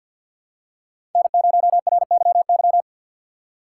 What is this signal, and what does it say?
Morse code, heard as a keyed tone, about 53 wpm, 700 Hz. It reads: N0RYY
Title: USA Call Signs - 4x Character Spacing - 50wpm